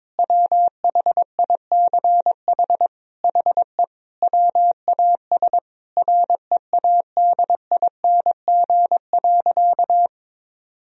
Morse code, heard as a keyed tone, about 22 wpm, 700 Hz. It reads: WHICH HE WAS READING.
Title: Sentences from Top 1000 Words - Review Part 3 - 22wpm